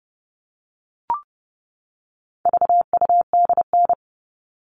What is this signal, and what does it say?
Morse code, heard as a keyed tone, about 30 wpm, 700 Hz. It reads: VUDN
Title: All Cut Numbers - 4-Digit 30wpm